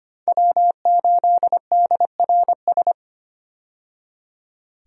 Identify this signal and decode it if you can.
Morse code, heard as a keyed tone, about 25 wpm, 700 Hz. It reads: W8DRS
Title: USA Call Signs - 5 Characters - 25wpm